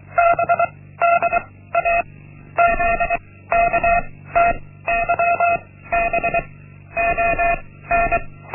Morse code, heard as a keyed tone, about 24 wpm, 690 Hz. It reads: B D A Z K T Y B O N
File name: Linksmakalnis-1